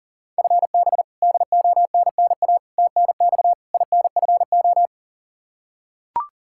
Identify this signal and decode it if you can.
Morse code, heard as a keyed tone, about 40 wpm, 700 Hz. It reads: FB DONNA TNX INFO
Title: Practice QSO - Exchange 3 - More Info - 40wpm